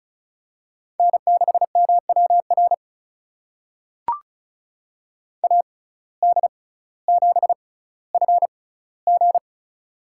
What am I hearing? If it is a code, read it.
Morse code, heard as a keyed tone, about 34 wpm, spaced as for 18 wpm, 700 Hz. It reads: N6MWR AD7FG
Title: USA Call Signs - 5 Characters - 6x Character Spacing - 35wpm